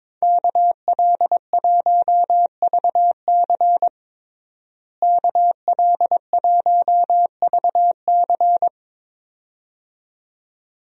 Morse code, heard as a keyed tone, about 22 wpm, 700 Hz. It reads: KL1VC KL1VC
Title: USA Call Signs - 5 Characters - Repeated 2x - 22wpm